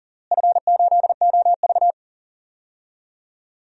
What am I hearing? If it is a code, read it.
Morse code, heard as a keyed tone, about 40 wpm, 700 Hz. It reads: F8OV
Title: DXCC-100 Call Signs - 4 Characters - Repeated 3x - 40wpm